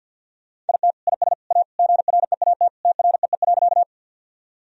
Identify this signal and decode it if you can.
Morse code, heard as a keyed tone, about 51 wpm, 700 Hz. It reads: IT IS A GREAT TREE.
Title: Sentences from Top 1000 Words - Review Part 1 - 50wpm